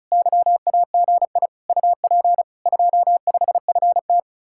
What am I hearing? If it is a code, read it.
Morse code, heard as a keyed tone, about 43 wpm, 700 Hz. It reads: YAGI UP 25FT
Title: Speed-Racing Common QSO Phrases - 35wpm